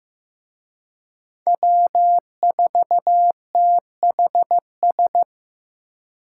Morse code, heard as a keyed tone, about 18 wpm, 700 Hz. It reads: W4THS
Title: Speed Racing - USA 1x3 Call Signs 15wpm